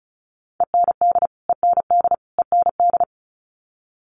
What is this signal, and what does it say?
Morse code, heard as a keyed tone, about 34 wpm, 700 Hz. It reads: END END END
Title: Top 500 Words Repeated 35wpm